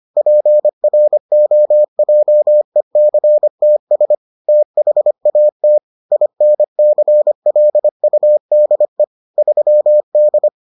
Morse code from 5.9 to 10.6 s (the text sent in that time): INCLUDE 3D